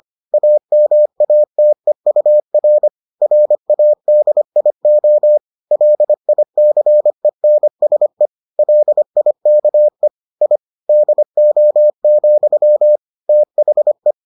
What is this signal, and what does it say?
Morse code, heard as a keyed tone, about 25 wpm, 600 Hz. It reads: AMATEUR RADIO LICENSE LIKE I DO, THE